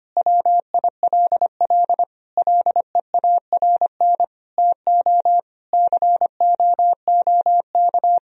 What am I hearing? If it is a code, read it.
Morse code, heard as a keyed tone, about 25 wpm, 700 Hz. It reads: WILL LEARN TO COOK
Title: Sentences from Top 1000 Words - Review Part 4 - 25wpm